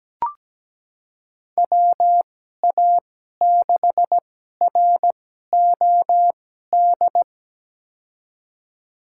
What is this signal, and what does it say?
Morse code, heard as a keyed tone, about 17 wpm, spaced as for 12 wpm, 700 Hz. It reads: WA6ROD
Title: USA Call Signs - 6 Characters - 2x Character Spacing - 17wpm